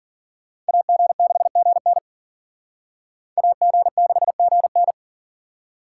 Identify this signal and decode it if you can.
Morse code, heard as a keyed tone, about 49 wpm, 700 Hz. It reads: AG6GN AG6GN
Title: Speed Racing - USA 2x2 Call Signs 40wpm